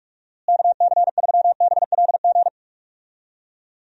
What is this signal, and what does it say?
Morse code, heard as a keyed tone, about 43 wpm, 700 Hz. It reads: KC3BLG
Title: USA Call Signs - 6 Characters - Repeated 3x - 45wpm